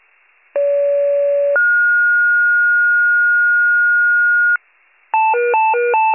Two-tone page, heard as a one-second tone, then a three-second tone, about 570 then 1430 Hz. It is heard from 0.6 to 4.6 s.